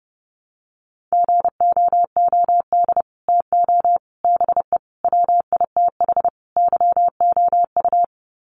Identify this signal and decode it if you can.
Morse code, heard as a keyed tone, about 30 wpm, 700 Hz. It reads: GOOD TO BE WITH YOU
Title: Sets of 5 Words Encore2 30wpm